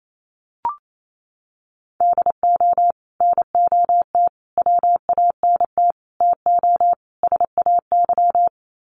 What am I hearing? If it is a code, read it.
Morse code, heard as a keyed tone, about 28 wpm, 700 Hz. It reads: DO NOT WANT TO SAY
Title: Sets of 5 Words Encore 28wpm